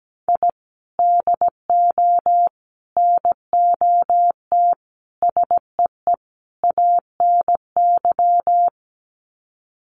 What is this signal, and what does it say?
Morse code, heard as a keyed tone, about 17 wpm, 700 Hz. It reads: I DO NOT SEE ANY
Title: Sets of 5 Words 17wpm